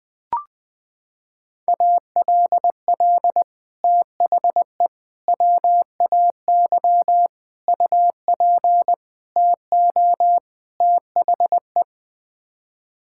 Morse code, heard as a keyed tone, about 20 wpm, 700 Hz. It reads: ALL THE WAY UP TO THE
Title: Sets of 6 Words - 20wpm